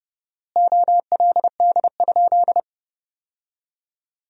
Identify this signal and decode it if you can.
Morse code, heard as a keyed tone, about 29 wpm, 700 Hz. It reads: OLD?
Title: 4x Word-Spacing - Sentences from Top 400 Words - 30wpm